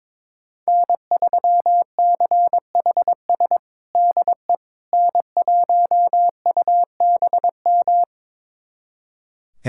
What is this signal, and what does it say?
Morse code, heard as a keyed tone, about 21 wpm, 700 Hz. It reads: N3CHS DE N1UBM
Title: USA Callsign DE Callsign - 5 Characters - 22wpm